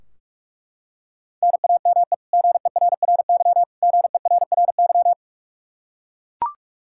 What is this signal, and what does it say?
Morse code, heard as a keyed tone, about 41 wpm, 700 Hz. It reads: NAME GERRY GERRY
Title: Practice QSO - Exchange 1 - Beginning - 45wpm